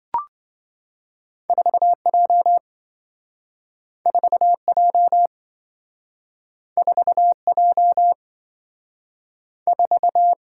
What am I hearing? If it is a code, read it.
Morse code, heard as a keyed tone, about 27 wpm, 700 Hz. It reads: 4J 4J 4J 4